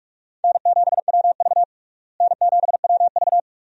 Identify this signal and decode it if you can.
Morse code, heard as a keyed tone, about 42 wpm, 700 Hz. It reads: N7WV N7WV
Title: USA Call Signs - 4 Characters - Repeated 2x - 45wpm